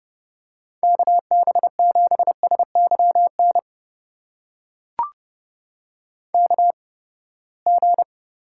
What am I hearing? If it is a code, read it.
Morse code, heard as a keyed tone, about 29 wpm, spaced as for 21 wpm, 700 Hz. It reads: KB7SYN KG
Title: USA Call Signs - 6 Characters - 8x Character Spacing - 30wpm